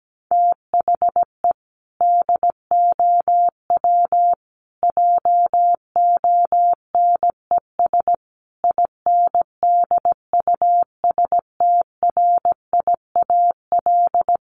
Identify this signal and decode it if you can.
Morse code, heard as a keyed tone, about 17 wpm, 700 Hz. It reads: THE DOW JONES INDUSTRIAL
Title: Sets of 5 Words 17wpm